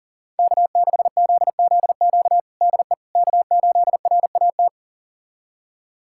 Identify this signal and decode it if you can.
Morse code, heard as a keyed tone, about 40 wpm, 700 Hz. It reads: K6ZZQ DE K8RAT
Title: USA Callsign DE Callsign 40wpm